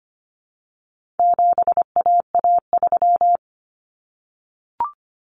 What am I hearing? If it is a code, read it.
Morse code, heard as a keyed tone, about 25 wpm, 700 Hz. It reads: 7AA3
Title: Common Cut Numbers - 4-Digit 25wpm